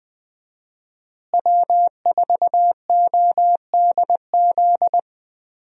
Morse code, heard as a keyed tone, about 20 wpm, 700 Hz. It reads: W4ODZ
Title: USA Call Signs - 5 Characters - Repeated 3x - 20wpm